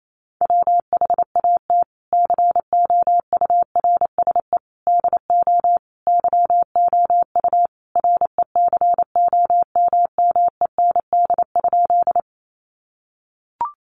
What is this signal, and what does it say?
Morse code, heard as a keyed tone, about 28 wpm, 700 Hz. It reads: WHAT COURSE DO YOU RECOMMEND?
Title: Sentences from Top 1200 Words 28wpm